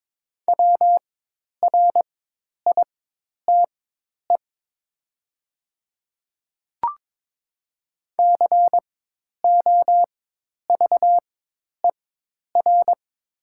Morse code, heard as a keyed tone, about 22 wpm, spaced as for 10 wpm, 700 Hz. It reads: WRITE COVER
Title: 4x Character Spacing - Top 300 Words - 22wpm